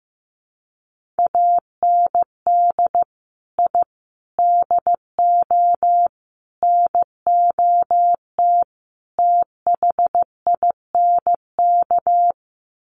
Morse code, heard as a keyed tone, about 15 wpm, 700 Hz. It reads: AND I DO NOT THINK
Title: Sets of 5 Words 15wpm